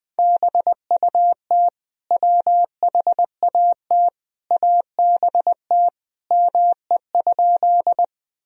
Morse code, heard as a keyed tone, about 20 wpm, 700 Hz. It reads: BUT WHAT ABT ME?
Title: Sentences from Top 200 Words and Abbreviations 20wpm